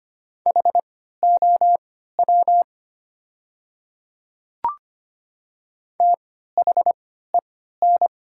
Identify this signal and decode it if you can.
Morse code, heard as a keyed tone, about 25 wpm, spaced as for 14 wpm, 700 Hz. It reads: HOW THEN
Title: 3x Character Spacing - Top 100 Words - 25wpm